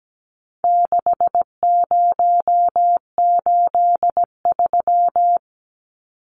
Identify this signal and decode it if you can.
Morse code, heard as a keyed tone, about 17 wpm, 700 Hz. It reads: 6083
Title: Numbers - 4-Digit 17wpm